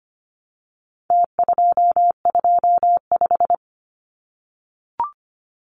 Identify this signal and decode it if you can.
Morse code, heard as a keyed tone, about 25 wpm, 700 Hz. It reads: T225
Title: Numbers - Contest Style - 4-Digit 25wpm